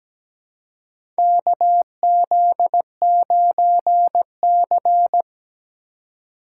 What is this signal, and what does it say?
Morse code, heard as a keyed tone, about 17 wpm, 700 Hz. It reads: KZ9C
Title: USA Call Signs - 4 Characters - 4x Character Spacing - 17wpm